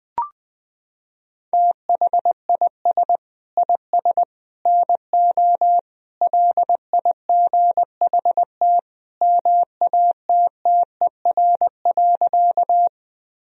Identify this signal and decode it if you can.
Morse code, heard as a keyed tone, about 20 wpm, 700 Hz. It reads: THIS IS NO LIGHT MATTER.
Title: Sentences from Top 1000 Words - Review Part 1 - 20wpm